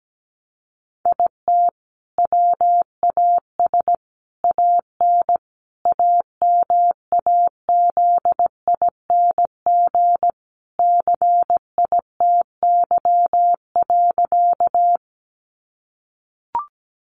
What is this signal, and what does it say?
Morse code, heard as a keyed tone, about 17 wpm, 700 Hz. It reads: IT WAS AN AMAZING CITY.
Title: Sentences from Top 1500 Words 17wpm